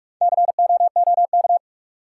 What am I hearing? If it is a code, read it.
Morse code, heard as a keyed tone, about 42 wpm, 700 Hz. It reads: COOK
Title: Sentences from Top 700 Words - New Word Drill - 45wpm